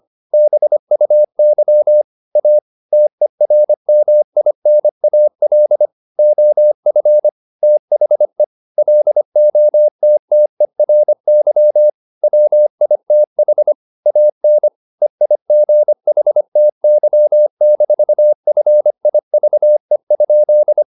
Morse code, heard as a keyed tone, about 25 wpm, 600 Hz. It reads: BUY A TERMINAL OF THE LOTTERY WITH AN EIGHTY-FIVE?
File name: the old man and the sea part 1 25wpm